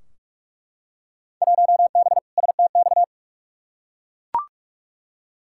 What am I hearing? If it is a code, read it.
Morse code, heard as a keyed tone, about 41 wpm, 700 Hz. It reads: 1B STX
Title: Field Day Exchanges Encore - 45wpm